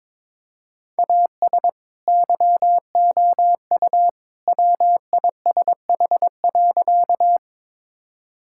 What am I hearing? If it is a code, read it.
Morse code, heard as a keyed tone, about 22 wpm, 700 Hz. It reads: AS YOU WISH.
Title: Sentences from Top 1600 Words 22wpm